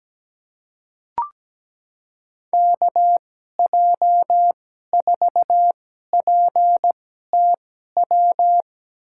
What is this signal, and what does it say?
Morse code, heard as a keyed tone, about 17 wpm, spaced as for 12 wpm, 700 Hz. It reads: KJ4PTW
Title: USA Call Signs - 6 Characters - 2x Character Spacing - 17wpm